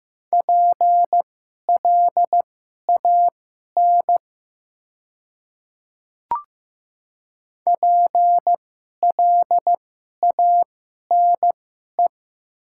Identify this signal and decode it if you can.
Morse code, heard as a keyed tone, about 15 wpm, spaced as for 11 wpm, 700 Hz. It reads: PLAN PLANE
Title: 2x Character Spacing - Top 500 Words - 15wpm